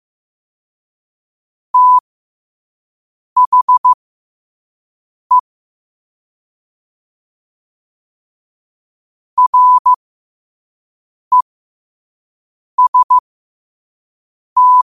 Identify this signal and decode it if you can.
Morse code, heard as a keyed tone, about 15 wpm, spaced as for 5 wpm, 1000 Hz. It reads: THE REST
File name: qotdfiles / 5wpm